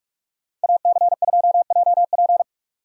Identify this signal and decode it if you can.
Morse code, heard as a keyed tone, about 42 wpm, 700 Hz. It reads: AC2JP